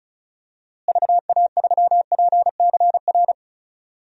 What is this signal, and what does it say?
Morse code, heard as a keyed tone, about 34 wpm, 700 Hz. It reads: VA3PCR
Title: DXCC-100 Call Signs - 6 Characters - Repeated 3x - 35wpm